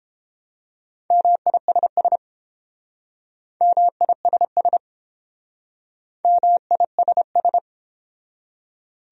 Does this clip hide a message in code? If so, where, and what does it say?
Yes: Morse code from 1.1 to 7.6 s, heard as a keyed tone, about 30 wpm, 700 Hz. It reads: MISS MISS MISS